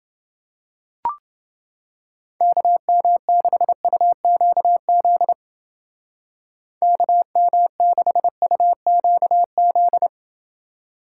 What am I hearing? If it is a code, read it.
Morse code, heard as a keyed tone, about 29 wpm, 700 Hz. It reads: KM6UQZ KM6UQZ